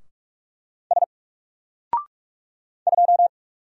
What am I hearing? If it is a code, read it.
Morse code, heard as a keyed tone, about 44 wpm, 700 Hz. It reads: S 2
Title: Single Letter-Number Alt 45wpm